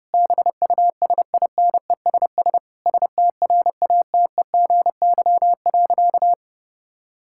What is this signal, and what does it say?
Morse code, heard as a keyed tone, about 32 wpm, 700 Hz. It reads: BUSINESS STRATEGY.
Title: Sentences from Top 1000 Words - Review Part 6 - 30wpm